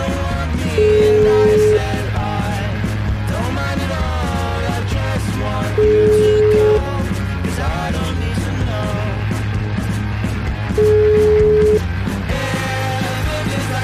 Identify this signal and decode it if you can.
Ringback tone (Europe, 425 Hz), with noise.